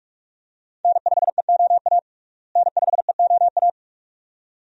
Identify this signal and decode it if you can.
Morse code, heard as a keyed tone, about 41 wpm, 700 Hz. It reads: N5EOA N5EOA
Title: USA Call Signs - 5 Characters - Repeated 2x - 45wpm